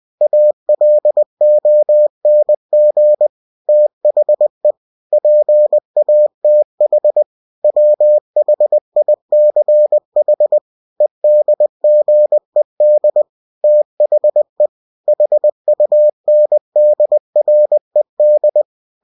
Morse code, heard as a keyed tone, about 20 wpm, 600 Hz. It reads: ALONG THE PATH WHICH EDGED THE HUNDRED
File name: cw-books / winnie-the-pooh / winnie-20wpm-page-0034